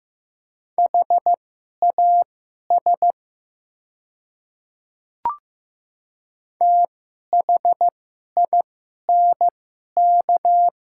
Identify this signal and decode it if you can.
Morse code, heard as a keyed tone, about 15 wpm, spaced as for 11 wpm, 700 Hz. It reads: HAS THINK
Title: 2x Character Spacing - Top 200 Words - 15wpm